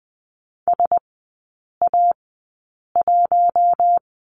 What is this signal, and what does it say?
Morse code, heard as a keyed tone, about 20 wpm, spaced as for 8 wpm, 700 Hz. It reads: SA1
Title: 047 Lesson 9e - Number 4 - Three Characters 20wpm